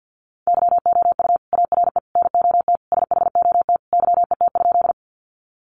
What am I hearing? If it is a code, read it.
Morse code, heard as a keyed tone, about 50 wpm, 700 Hz. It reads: YOU ARE NOT SHOT YET?
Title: Sentences from Top 1100 Words 50wpm